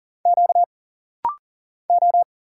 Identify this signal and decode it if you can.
Morse code, heard as a keyed tone, about 40 wpm, 700 Hz. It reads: Q O
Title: Single Letter-Number Alt - Rapid-Fire 40wpm